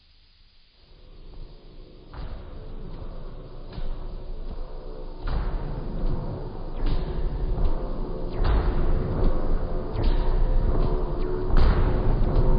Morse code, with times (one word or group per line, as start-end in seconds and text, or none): none